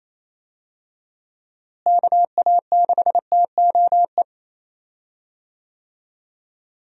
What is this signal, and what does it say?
Morse code, heard as a keyed tone, about 27 wpm, 700 Hz. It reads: KA6TOE
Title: USA Call Signs - 6 Characters - Repeated 2x - 28wpm